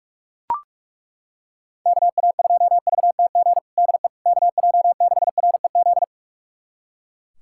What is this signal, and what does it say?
Morse code, heard as a keyed tone, about 42 wpm, 700 Hz. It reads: KA2VTG DE KJ6REZ
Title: USA Callsign DE Callsign - 6 Characters - 45wpm